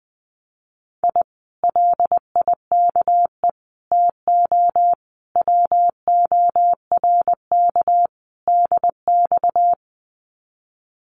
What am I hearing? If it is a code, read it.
Morse code, heard as a keyed tone, about 20 wpm, 700 Hz. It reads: I LIKE TO WORK DX
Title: Common QSO Phrases Encore 20wpm